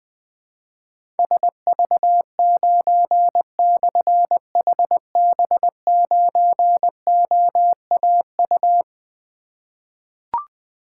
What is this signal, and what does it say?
Morse code, heard as a keyed tone, about 21 wpm, 700 Hz. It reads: SV9/HB9OAU
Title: Difficult Call Signs 20wpm